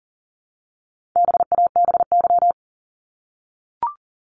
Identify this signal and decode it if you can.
Morse code, heard as a keyed tone, about 40 wpm, 700 Hz. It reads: BABY
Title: Top 1001-1250 Words 40wpm